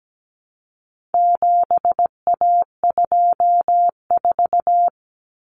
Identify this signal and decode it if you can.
Morse code, heard as a keyed tone, about 17 wpm, 700 Hz. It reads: 7A24
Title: Common Cut Numbers - 4-Digit 17wpm